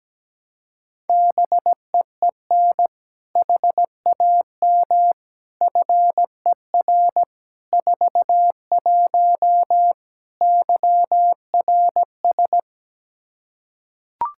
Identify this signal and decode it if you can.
Morse code, heard as a keyed tone, about 17 wpm, 700 Hz. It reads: BEEN HAM FER 41 YRS
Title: Practice QSO - Exchange 3 - More Info - 17wpm